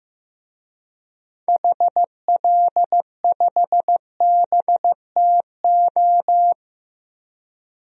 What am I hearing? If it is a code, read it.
Morse code, heard as a keyed tone, about 15 wpm, 700 Hz. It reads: HL5BTO
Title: DXCC-100 Call Signs - 6 Characters - 15wpm